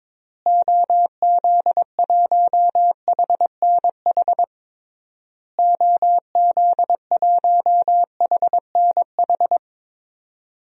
Morse code, heard as a keyed tone, about 22 wpm, 700 Hz. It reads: OZ1HNH OZ1HNH
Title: DXCC-100 Call Signs - 6 Characters - Repeated 3x - 22wpm